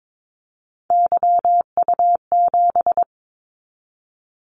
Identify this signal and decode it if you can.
Morse code, heard as a keyed tone, about 22 wpm, 700 Hz. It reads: YU7
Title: Callsign Prefixes for Top 25 DXCC - 22wpm